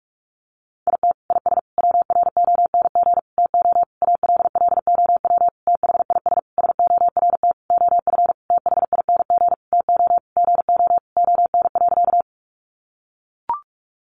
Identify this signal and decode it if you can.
Morse code, heard as a keyed tone, about 44 wpm, 700 Hz. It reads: IT IS WRONG TO ALLOW THIS SORT OF THING TO GO ON.
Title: Sentences from Top 1300 Words 45wpm